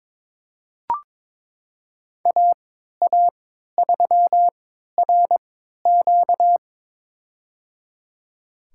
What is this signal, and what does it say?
Morse code, heard as a keyed tone, about 22 wpm, spaced as for 13 wpm, 700 Hz. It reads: AA3RQ
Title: USA Call Signs - 5 Characters - 3x Character Spacing - 22wpm